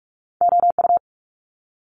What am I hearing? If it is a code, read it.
Morse code, heard as a keyed tone, about 46 wpm, 700 Hz. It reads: OU
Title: Multi-letter Phonemes Quick Fire 45wpm